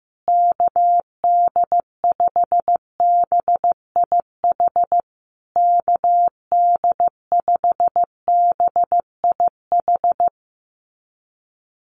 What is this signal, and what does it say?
Morse code, heard as a keyed tone, about 15 wpm, 700 Hz. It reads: KD5BIH KD5BIH
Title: USA Call Signs Repeated 3x 15wpm